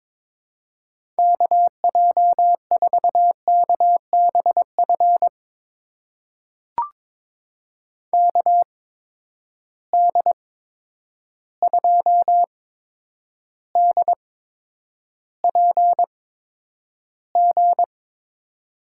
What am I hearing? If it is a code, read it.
Morse code, heard as a keyed tone, about 22 wpm, spaced as for 9 wpm, 700 Hz. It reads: KJ4KBF KD2DPG
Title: USA Call Signs - 8x Character Spacing - 22wpm